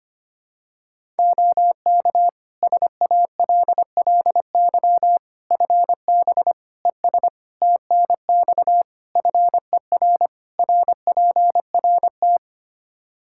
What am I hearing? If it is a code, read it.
Morse code, heard as a keyed tone, about 25 wpm, 700 Hz. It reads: OK SALLY FB ES TNX FER RPRT
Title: Practice QSO - Exchange 2 - Info - 25wpm